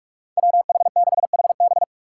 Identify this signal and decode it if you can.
Morse code, heard as a keyed tone, about 46 wpm, 690 Hz. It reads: WH6HB
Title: USA Call Signs - 5 Characters - 2x Character Spacing - 45wpm